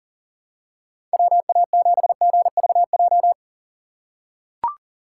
Morse code, heard as a keyed tone, about 38 wpm, 700 Hz. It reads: WA7GVJ
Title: USA Call Signs - 6 Characters - 6x Character Spacing - 40wpm